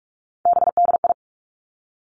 Morse code, heard as a keyed tone, about 46 wpm, 700 Hz. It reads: BDI
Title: Three Challenging Letter-Number Combinations 45wpm